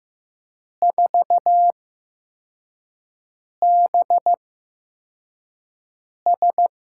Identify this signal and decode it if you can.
Morse code, heard as a keyed tone, about 15 wpm, spaced as for 4 wpm, 700 Hz. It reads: 4BS
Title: USA Call Signs - 6 Characters - 8x Character Spacing - 15wpm